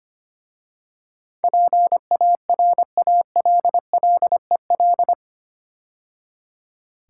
Morse code, heard as a keyed tone, about 25 wpm, 700 Hz. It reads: PARALLEL
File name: 8 Letter Words - First 3 Letters Given - 25wpm